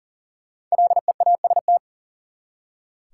Morse code, heard as a keyed tone, about 40 wpm, 700 Hz. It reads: LEAST
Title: Five Letter Words - 40wpm